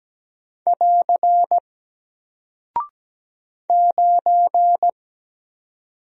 Morse code, heard as a keyed tone, about 17 wpm, 700 Hz. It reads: <AR> 9
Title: Single Character (Letter:Number:Punctuation:Prosign) 17wpm